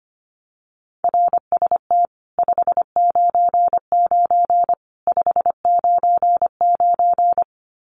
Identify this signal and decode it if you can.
Morse code, heard as a keyed tone, about 25 wpm, 700 Hz. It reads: RST 599 599
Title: Common QSO Phrases 25wpm